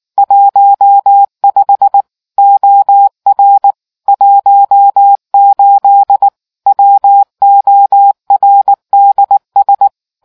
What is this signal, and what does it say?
Morse code, heard as a keyed tone, about 19 wpm, 800 Hz. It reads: 15 OR 18 WORDS